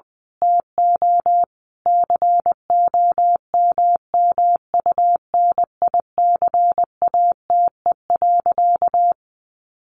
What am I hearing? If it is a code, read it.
Morse code, heard as a keyed tone, about 20 wpm, 700 Hz. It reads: TO COMMUNICATE.
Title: Sentences from Top 1300 Words 20wpm